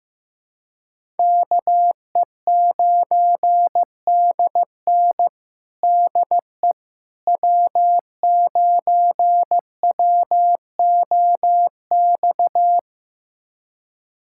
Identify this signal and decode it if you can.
Morse code, heard as a keyed tone, about 15 wpm, 700 Hz. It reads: KE9DN DE W9WOX
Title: USA Callsign DE Callsign - 5 Characters - 15wpm